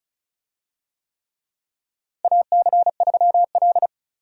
Morse code, heard as a keyed tone, about 36 wpm, 700 Hz. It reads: AC3L
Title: USA Call Signs - 4 Characters - Repeated 3x - 35wpm